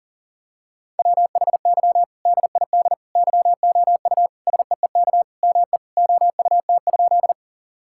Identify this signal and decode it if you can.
Morse code, heard as a keyed tone, about 37 wpm, 700 Hz. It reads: WHY DID YOU SEEK ME OUT?
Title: Sentences from Top 1600 Words 40wpm